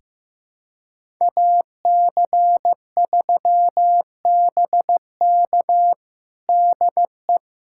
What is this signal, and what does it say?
Morse code, heard as a keyed tone, about 16 wpm, 700 Hz. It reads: AC3BK DE
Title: USA Callsign DE Callsign - 5 Characters - 15wpm